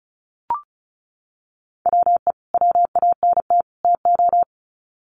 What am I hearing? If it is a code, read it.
Morse code, heard as a keyed tone, about 34 wpm, 700 Hz. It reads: WE WANT TO
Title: Sets of 3 Words 35wpm